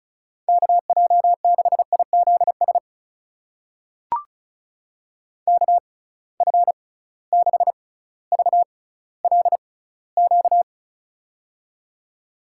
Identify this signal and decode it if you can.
Morse code, heard as a keyed tone, about 36 wpm, spaced as for 18 wpm, 700 Hz. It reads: KJ6IZS KF6VLQ
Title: USA Call Signs - 6 Characters - 6x Character Spacing - 35wpm